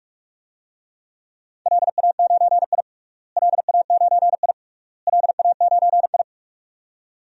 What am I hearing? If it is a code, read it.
Morse code, heard as a keyed tone, about 47 wpm, 700 Hz. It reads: LA9I LA9I LA9I
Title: DXCC-100 Call Signs - 4 Characters - Repeated 3x - 45wpm